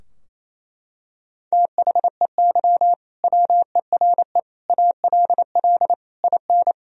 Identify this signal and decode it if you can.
Morse code, heard as a keyed tone, about 28 wpm, 700 Hz. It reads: THEY WERE ALL IN
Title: Sentences from Top 1000 Words - Review Part 3 - 28wpm